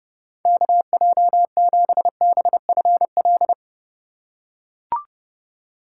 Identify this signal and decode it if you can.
Morse code, heard as a keyed tone, about 32 wpm, 700 Hz. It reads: KJ7BFL
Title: USA Call Signs - 6 Characters - 2x Character Spacing - 30wpm